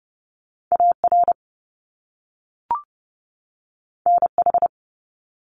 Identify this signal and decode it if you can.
Morse code, heard as a keyed tone, about 30 wpm, 700 Hz. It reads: AR NH
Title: US States 30wpm